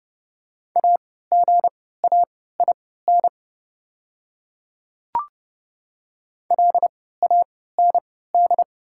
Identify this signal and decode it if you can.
Morse code, heard as a keyed tone, about 30 wpm, spaced as for 17 wpm, 700 Hz. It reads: AGAIN LAND
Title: 3x Character Spacing - Top 200 Words - 30wpm